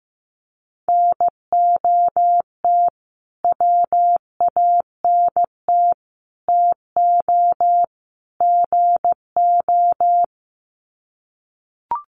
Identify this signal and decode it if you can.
Morse code, heard as a keyed tone, about 15 wpm, 700 Hz. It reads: NOT WANT TO GO
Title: Sets of 4 Words 15wpm